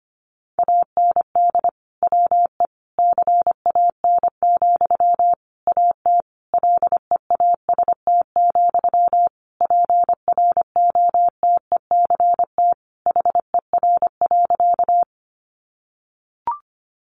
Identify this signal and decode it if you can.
Morse code, heard as a keyed tone, about 25 wpm, 700 Hz. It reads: AND WE CAN, AT LEAST, PROTECT HER.
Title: Sentences from Top 900 Words 25wpm